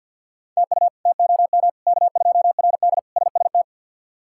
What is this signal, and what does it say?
Morse code, heard as a keyed tone, about 48 wpm, 700 Hz. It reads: TU TOM K2RD SST
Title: K1USN SST QSO Practice - 50wpm